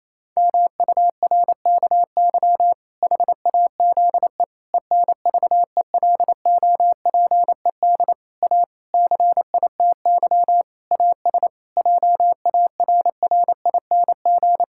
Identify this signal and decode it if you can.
Morse code, heard as a keyed tone, about 27 wpm, 700 Hz. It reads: MURKY HAZE ENVELOPED A CITY AS JARRING
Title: Pangram Challenge v2 - 28wpm